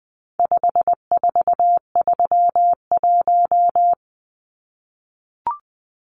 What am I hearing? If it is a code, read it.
Morse code, heard as a keyed tone, about 20 wpm, 700 Hz. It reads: 5431
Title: Numbers - 4-Digit 20wpm